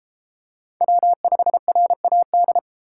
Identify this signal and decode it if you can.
Morse code, heard as a keyed tone, about 31 wpm, 700 Hz. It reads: W5RAD